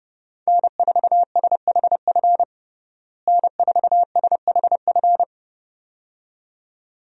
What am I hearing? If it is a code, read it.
Morse code, heard as a keyed tone, about 31 wpm, 700 Hz. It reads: N4SHF N4SHF
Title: USA Call Signs - 5 Characters - Repeated 2x - 30wpm